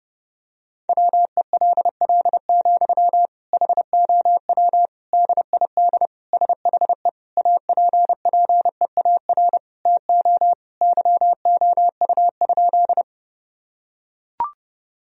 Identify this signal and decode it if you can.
Morse code, heard as a keyed tone, about 30 wpm, 700 Hz. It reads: WELL, HOW DID SHE APPEAR TO YOU?
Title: Sentences from Top 1000 Words - Review Part 2 - 30wpm